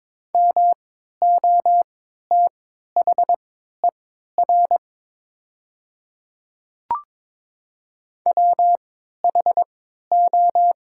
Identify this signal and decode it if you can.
Morse code, heard as a keyed tone, about 22 wpm, spaced as for 12 wpm, 700 Hz. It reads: MOTHER WHO
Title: 3x Character Spacing - Top 200 Words - 22wpm